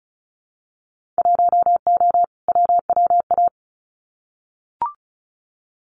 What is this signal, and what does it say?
Morse code, heard as a keyed tone, about 34 wpm, 700 Hz. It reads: 1O WWA
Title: Winter Field Day Exchanges 35wpm